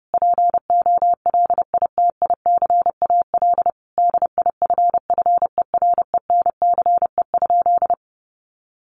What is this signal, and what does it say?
Morse code, heard as a keyed tone, about 30 wpm, 700 Hz. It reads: POLITICAL DIFFERENCE?
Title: Sentences from Top 800 Words 30wpm